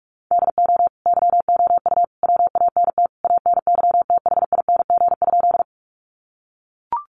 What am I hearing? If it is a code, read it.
Morse code, heard as a keyed tone, about 44 wpm, 700 Hz. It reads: DO YOU WANT ANYTHING?
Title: Sentences from Top 1000 Words 45wpm